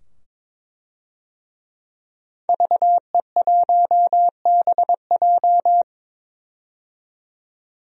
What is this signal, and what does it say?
Morse code, heard as a keyed tone, about 22 wpm, 700 Hz. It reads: VE1BJ
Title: DXCC-100 Call Signs - 5 Characters - 22wpm